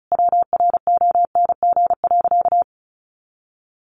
Morse code, heard as a keyed tone, about 34 wpm, 700 Hz. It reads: WRONG.
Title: Sentences from Top 900 Words 35wpm